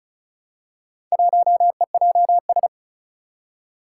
Morse code, heard as a keyed tone, about 35 wpm, 700 Hz. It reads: 1EJS
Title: Four Challenging Letter-Number Combinations 35wpm